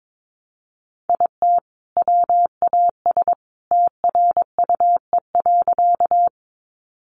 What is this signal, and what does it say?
Morse code, heard as a keyed tone, about 22 wpm, 700 Hz. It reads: IT WAS TRUE.
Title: Sentences from Top 400 Words 22wpm